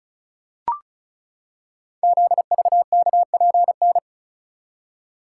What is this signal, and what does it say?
Morse code, heard as a keyed tone, about 36 wpm, 690 Hz. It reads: ZVKPN
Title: Code Groups - 5 Letters - 35wpm